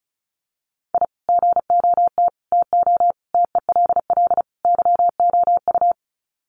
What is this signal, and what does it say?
Morse code, heard as a keyed tone, about 34 wpm, 700 Hz. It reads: I GOT TO TELL YOU
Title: Sets of 5 Words Encore2 35wpm